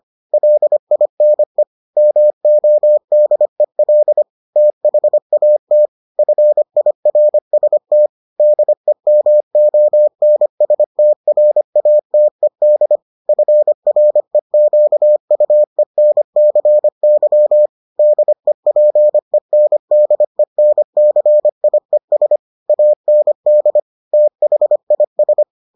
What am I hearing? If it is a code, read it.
Morse code, heard as a keyed tone, about 25 wpm, 600 Hz. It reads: LINE MODEL THAT FIRST DEMONSTRATED FREQUENCY DEPENDENCIES AND THIS